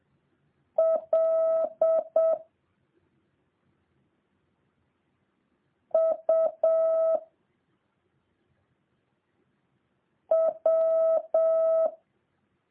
Morse code, heard as a keyed tone, about 7 wpm, 650 Hz. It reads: L U W